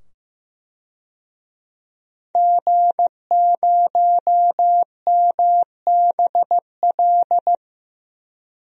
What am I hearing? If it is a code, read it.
Morse code, heard as a keyed tone, about 15 wpm, 700 Hz. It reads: G0MBL
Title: DXCC-100 Call Signs - 5 Characters - 15wpm